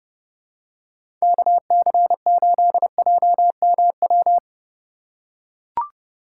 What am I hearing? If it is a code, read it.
Morse code, heard as a keyed tone, about 30 wpm, 700 Hz. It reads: KC8JMW